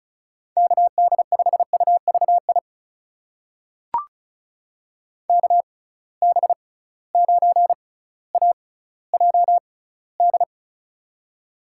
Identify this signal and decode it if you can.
Morse code, heard as a keyed tone, about 36 wpm, spaced as for 18 wpm, 700 Hz. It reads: KD5UVI KB9AJD
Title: USA Call Signs - 6 Characters - 6x Character Spacing - 35wpm